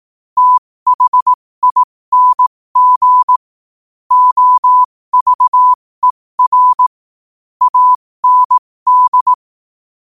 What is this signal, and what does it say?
Morse code, heard as a keyed tone, about 18 wpm, spaced as for 15 wpm, 1000 Hz. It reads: THING OVER AND
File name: qotdfiles / 15wpm